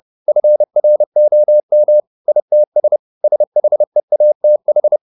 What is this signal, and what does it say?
Morse code, heard as a keyed tone, about 30 wpm, 600 Hz. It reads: FROM ITS SHEATH